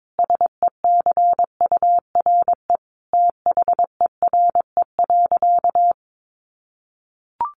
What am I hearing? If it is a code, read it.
Morse code, heard as a keyed tone, about 22 wpm, 700 Hz. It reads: SECURE THERE.
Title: Sentences from Top 1200 Words 22wpm